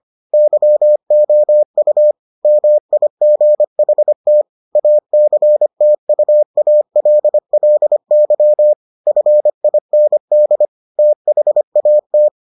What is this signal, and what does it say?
Morse code, heard as a keyed tone, about 25 wpm, 600 Hz. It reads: YOU MIGHT ACTUALLY FIND THAT